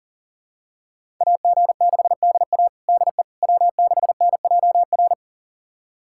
Morse code, heard as a keyed tone, about 40 wpm, 700 Hz. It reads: AG6DA DE W6NJR
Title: USA Callsign DE Callsign - 5 Characters - 40wpm